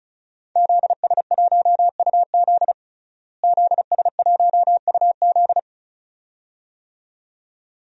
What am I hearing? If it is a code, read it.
Morse code, heard as a keyed tone, about 36 wpm, 700 Hz. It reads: ZS1UZ ZS1UZ
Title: DXCC-100 Call Signs - 5 Characters - Repeated 2x - 35wpm